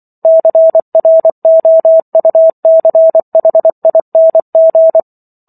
Morse code, heard as a keyed tone, about 24 wpm, 650 Hz. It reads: CROUCHING